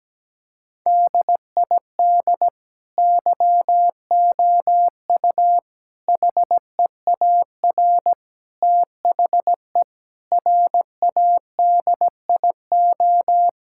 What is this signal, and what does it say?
Morse code, heard as a keyed tone, about 17 wpm, 700 Hz. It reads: DID YOU HEAR THE RADIO
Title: Sentences from Top 1000 Words - Review Part 5 - 17wpm